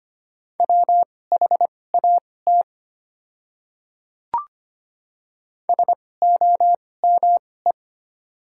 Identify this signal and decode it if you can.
Morse code, heard as a keyed tone, about 25 wpm, spaced as for 18 wpm, 700 Hz. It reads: WHAT SOME
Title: 2x Character Spacing - Top 200 Words - 25wpm